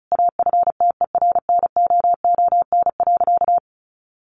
Morse code, heard as a keyed tone, about 34 wpm, 700 Hz. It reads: AFTERNOON.
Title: Sentences from Top 1500 Words 35wpm